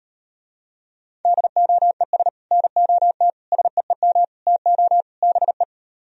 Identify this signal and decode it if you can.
Morse code, heard as a keyed tone, about 38 wpm, 700 Hz. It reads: DOES NOT SEEM TO BE